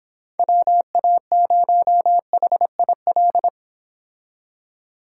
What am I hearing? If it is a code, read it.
Morse code, heard as a keyed tone, about 26 wpm, 700 Hz. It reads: WA0HIL